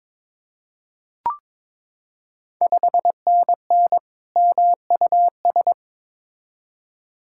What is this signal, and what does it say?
Morse code, heard as a keyed tone, about 21 wpm, 690 Hz. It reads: 5NN MUS
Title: Ontario QSO Party Exchange 22wpm